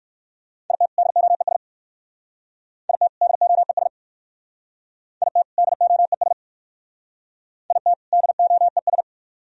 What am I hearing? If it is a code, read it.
Morse code, heard as a keyed tone, about 55 wpm, 700 Hz. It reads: IT DOES IT DOES IT DOES IT DOES
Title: Speed-Racing - Sets of 2 Words 45wpm